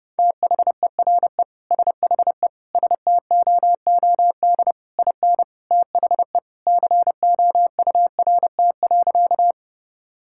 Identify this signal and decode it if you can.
Morse code, heard as a keyed tone, about 32 wpm, 700 Hz. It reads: THERE SHE STOOD IN THE COURT.
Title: Sentences from Top 1600 Words 30wpm